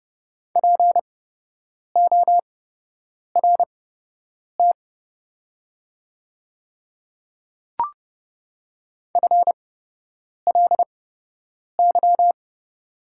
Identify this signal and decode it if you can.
Morse code, heard as a keyed tone, about 30 wpm, spaced as for 8 wpm, 700 Hz. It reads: PORT FLY
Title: 8x Character Spacing - Top 500 Words - 30wpm